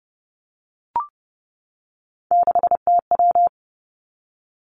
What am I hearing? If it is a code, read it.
Morse code, heard as a keyed tone, about 30 wpm, 700 Hz. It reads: 6TW
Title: Three Challenging Letter-Number Combinations 30wpm